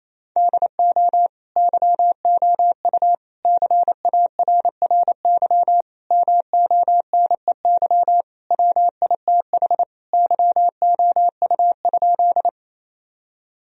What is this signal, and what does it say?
Morse code, heard as a keyed tone, about 28 wpm, 700 Hz. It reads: DO YOU CARRY MONEY WITH YOU?
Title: Sentences from Top 1000 Words - Review Part 1 - 28wpm